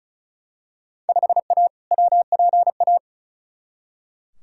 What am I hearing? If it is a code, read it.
Morse code, heard as a keyed tone, about 34 wpm, 700 Hz. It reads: 5A WPA
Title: Field Day Exchanges Encore - 35wpm